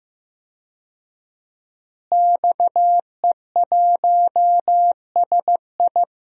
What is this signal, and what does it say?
Morse code, heard as a keyed tone, about 15 wpm, 700 Hz. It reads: XE1SI
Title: DXCC-100 Call Signs - 5 Characters - Repeated 3x - 15wpm